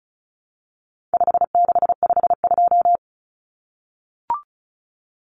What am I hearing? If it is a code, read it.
Morse code, heard as a keyed tone, about 34 wpm, 700 Hz. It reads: 5652
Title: Common Cut Numbers - 4-Digit 35wpm